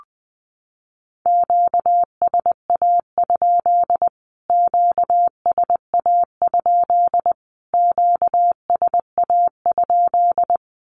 Morse code, heard as a keyed tone, about 20 wpm, 700 Hz. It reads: QSA? QSA? QSA?
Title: Q-Codes Repeated 3x 20wpm